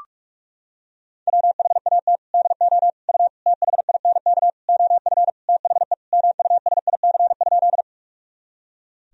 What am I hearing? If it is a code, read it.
Morse code, heard as a keyed tone, about 46 wpm, 700 Hz. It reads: WHAT DO U THINK OF THE MUSIC?
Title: Sentences from Top 300 Words and Abbreviations 45wpm